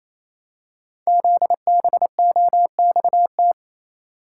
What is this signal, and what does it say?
Morse code, heard as a keyed tone, about 28 wpm, 700 Hz. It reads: ZBOXT
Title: Code Groups - 5 Letters - 28wpm